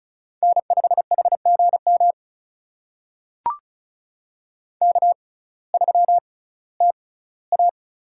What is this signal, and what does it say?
Morse code, heard as a keyed tone, about 36 wpm, spaced as for 19 wpm, 700 Hz. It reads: N5HGM K3TA
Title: USA Call Signs - 6x Character Spacing - 35wpm